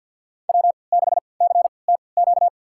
Morse code, heard as a keyed tone, about 50 wpm, spaced as for 28 wpm, 700 Hz. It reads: W6CTQ
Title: USA Call Signs - 5 Characters - 3x Character Spacing - 50wpm